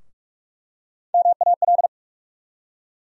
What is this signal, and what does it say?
Morse code, heard as a keyed tone, about 47 wpm, 700 Hz. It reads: MAL
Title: USA Call Sign Postfixes - 3 Letters 45wpm